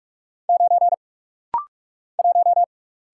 Morse code, heard as a keyed tone, about 45 wpm, 700 Hz. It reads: 9 1
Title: Single Letter-Number Alt - Rapid-Fire 45wpm